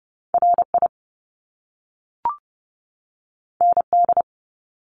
Morse code, heard as a keyed tone, about 30 wpm, 700 Hz. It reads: RI ND
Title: Top Two Letter Combinations in Words 30wpm